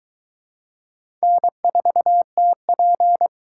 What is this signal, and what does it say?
Morse code, heard as a keyed tone, about 22 wpm, 700 Hz. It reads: N4TP